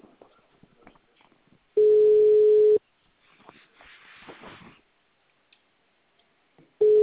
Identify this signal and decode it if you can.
Ringback tone (Europe, 425 Hz).
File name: out- -2012- - - .25488